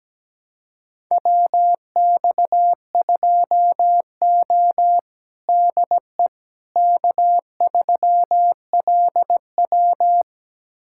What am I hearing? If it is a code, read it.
Morse code, heard as a keyed tone, about 17 wpm, 700 Hz. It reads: WX2O DE K3LW
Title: USA Callsign DE Callsign - 4 Characters - 17wpm